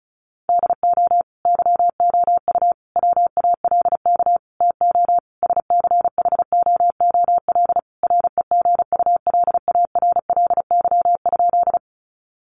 Morse code, heard as a keyed tone, about 34 wpm, 700 Hz. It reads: DO YOU WALK TO SCHOOL REGULARLY?
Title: Sentences from Top 1400 Words 35wpm